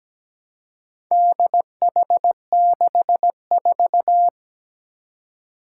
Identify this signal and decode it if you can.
Morse code, heard as a keyed tone, about 17 wpm, 700 Hz. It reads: DH64
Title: Four Challenging Letter-Number Combinations 17wpm